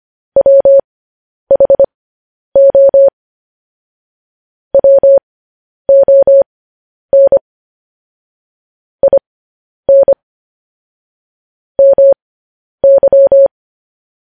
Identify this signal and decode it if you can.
Morse code, heard as a keyed tone, about 25 wpm, spaced as for 10 wpm, 550 Hz. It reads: WHO WON IN MY
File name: podcast / News headlines-1x-25x10